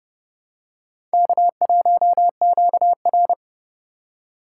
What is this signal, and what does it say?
Morse code, heard as a keyed tone, about 29 wpm, 700 Hz. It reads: K1QR